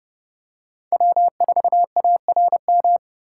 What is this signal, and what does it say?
Morse code, heard as a keyed tone, about 30 wpm, 700 Hz. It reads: W4ARM
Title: USA Call Signs - 5 Characters - 2x Character Spacing - 30wpm